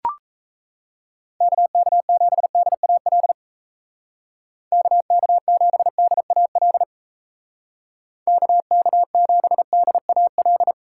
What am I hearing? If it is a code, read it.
Morse code, heard as a keyed tone, about 34 wpm, 700 Hz. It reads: KK7DAL KK7DAL KK7DAL
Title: Speed Racing - USA 2x3 Call Signs 28wpm